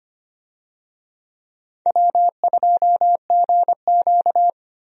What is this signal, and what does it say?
Morse code, heard as a keyed tone, about 25 wpm, 700 Hz. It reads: W2GQ
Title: USA Call Signs - 4 Characters - Repeated 3x - 25wpm